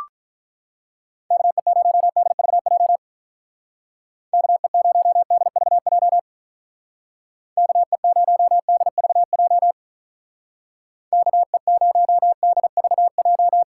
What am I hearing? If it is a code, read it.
Morse code, heard as a keyed tone, about 44 wpm, 700 Hz. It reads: KE0DVJ KE0DVJ KE0DVJ KE0DVJ